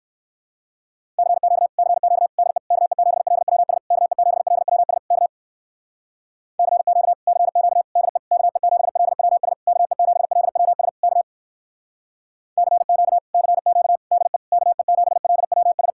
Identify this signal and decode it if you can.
Morse code, heard as a keyed tone, about 59 wpm, 700 Hz. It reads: CQ CQ DE KE7LWS KE7LWS K CQ CQ DE KE7LWS KE7LWS K CQ CQ DE KE7LWS